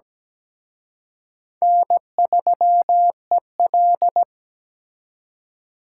Morse code, heard as a keyed tone, about 20 wpm, 700 Hz. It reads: N3EL
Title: Speed Racing - USA 1x2 Call Signs 17wpm